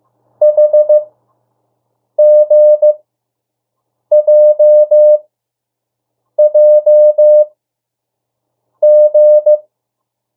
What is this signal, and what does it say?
Morse code, heard as a keyed tone, about 15 wpm, spaced as for 6 wpm, 600 Hz. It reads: HGJJG
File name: morse mp3 / 02 urok 1